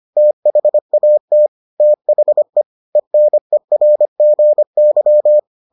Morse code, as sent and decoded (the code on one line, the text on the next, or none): - .... .- - / - .... / . -. .-. --. -.--
THAT THE ENERGY